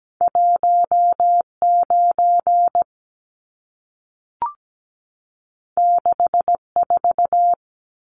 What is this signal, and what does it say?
Morse code, heard as a keyed tone, about 17 wpm, 700 Hz. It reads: 19 64
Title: Numbers - 2-Digit 17wpm